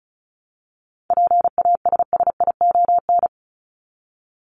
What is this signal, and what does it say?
Morse code, heard as a keyed tone, about 34 wpm, 700 Hz. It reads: PASSION
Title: Top 1001-1250 Words 35wpm